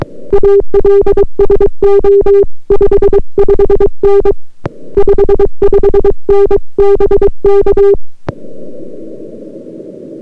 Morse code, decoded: ALSO 55N 55N BK